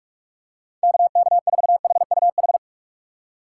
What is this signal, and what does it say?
Morse code, heard as a keyed tone, about 44 wpm, 700 Hz. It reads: KK4HUH
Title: USA Call Signs - 6 Characters - 6x Character Spacing - 45wpm